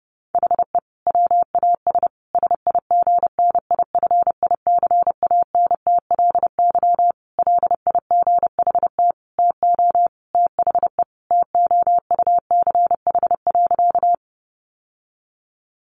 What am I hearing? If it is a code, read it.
Morse code, heard as a keyed tone, about 30 wpm, 700 Hz. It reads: HE WAS SIGNIFICANTLY LIGHT TO THE TOUCH.
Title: Sentences from Top 1500 Words 30wpm